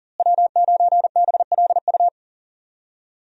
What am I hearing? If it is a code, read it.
Morse code, heard as a keyed tone, about 40 wpm, 700 Hz. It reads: W9BLU
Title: USA Call Signs - 5 Characters - Repeated 2x - 40wpm